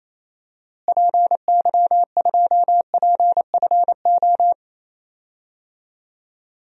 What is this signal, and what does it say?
Morse code, heard as a keyed tone, about 28 wpm, 700 Hz. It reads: PY2PFO
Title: DXCC-100 Call Signs - 6 Characters - Repeated 2x - 28wpm